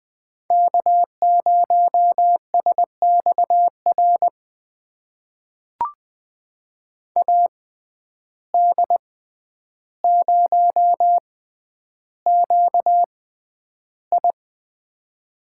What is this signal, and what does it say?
Morse code, heard as a keyed tone, about 20 wpm, spaced as for 10 wpm, 700 Hz. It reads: K0SXR AD0QI
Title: USA Call Signs - 5 Characters - 6x Character Spacing - 20wpm